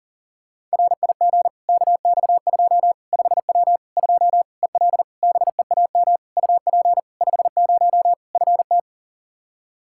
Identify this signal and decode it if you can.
Morse code, heard as a keyed tone, about 40 wpm, 700 Hz. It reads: RIG KX2 5W 2 EL BEAM UP 50 FT
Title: Practice QSO - Exchange 2 - Info - 40wpm